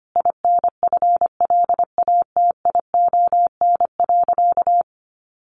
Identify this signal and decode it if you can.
Morse code, heard as a keyed tone, about 25 wpm, 700 Hz. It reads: INFLATION.
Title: Sentences from Top 1100 Words 25wpm